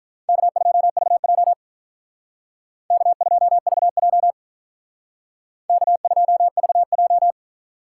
Morse code, heard as a keyed tone, about 43 wpm, 700 Hz. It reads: K2VJ K2VJ K2VJ
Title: Speed Racing - USA 1x2 Call Signs 35wpm